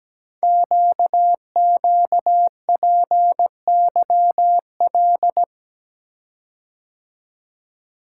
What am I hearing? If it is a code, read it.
Morse code, heard as a keyed tone, about 17 wpm, 700 Hz. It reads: QQPYL
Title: Code Groups - 5 Letters - 17wpm